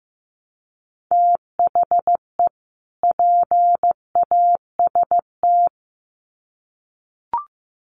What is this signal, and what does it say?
Morse code, heard as a keyed tone, about 15 wpm, 700 Hz. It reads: THE PAST
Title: Sets of 2 Words 15wpm